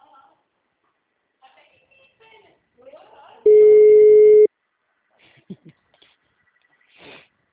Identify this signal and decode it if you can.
Ringback tone (Europe, 425 Hz), with speech.
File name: out- -2008- - - .1030